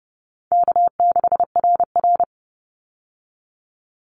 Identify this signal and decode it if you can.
Morse code, heard as a keyed tone, about 30 wpm, 700 Hz. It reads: K6RR
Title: Call Signs Repeated 2x - 30wpm